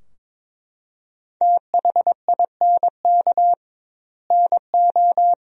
Morse code, heard as a keyed tone, about 22 wpm, 700 Hz. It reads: THINK NO
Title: 2x Word-Spacing - Sentences from Top 200 Words - 22wpm